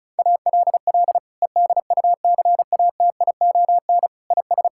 Morse code, heard as a keyed tone, about 36 wpm, 700 Hz. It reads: ALL EDUCATION IS
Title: Sentences from Top 1000 Words - Review Part 6 - 35wpm